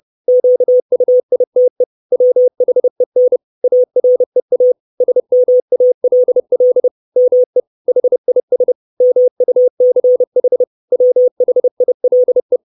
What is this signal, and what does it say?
Morse code, heard as a keyed tone, about 30 wpm, 500 Hz. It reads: QUITE WHEN AREA SMALL ME HIS MUCH WHILE
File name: 632 Words 30WPM